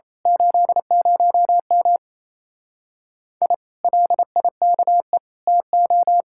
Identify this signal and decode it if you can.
Morse code, heard as a keyed tone, about 27 wpm, 700 Hz. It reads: 80M I LIKE TO
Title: Speed-Racing Common QSO Phrases Encore - 28wpm